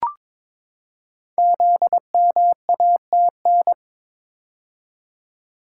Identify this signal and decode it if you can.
Morse code, heard as a keyed tone, about 22 wpm, 700 Hz. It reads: ZMATN
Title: Code Groups - 5 Letters - 22wpm